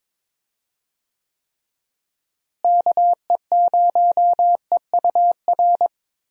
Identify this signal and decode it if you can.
Morse code, heard as a keyed tone, about 22 wpm, 700 Hz. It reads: KE0EUR
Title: USA Call Signs - 6 Characters - Repeated 2x - 22wpm